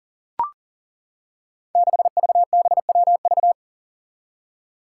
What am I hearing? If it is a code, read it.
Morse code, heard as a keyed tone, about 40 wpm, 700 Hz. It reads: 6VBWV
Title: Five Challenging Letter-Number Combinations 40wpm